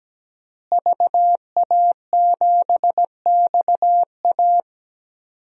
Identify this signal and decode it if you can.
Morse code, heard as a keyed tone, about 17 wpm, 700 Hz. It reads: VA7XA
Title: DXCC-100 Call Signs - 5 Characters - Repeated 3x - 17wpm